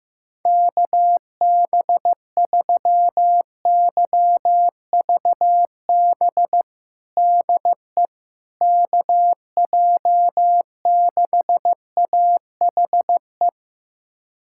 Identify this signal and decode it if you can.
Morse code, heard as a keyed tone, about 16 wpm, 700 Hz. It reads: KB3YVB DE KJ6AHE
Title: USA Callsign DE Callsign - 6 Characters - 15wpm